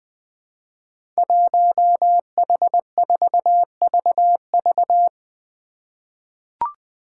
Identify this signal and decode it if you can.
Morse code, heard as a keyed tone, about 20 wpm, 700 Hz. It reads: 1H4VV
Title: Five Challenging Letter-Number Combinations 20wpm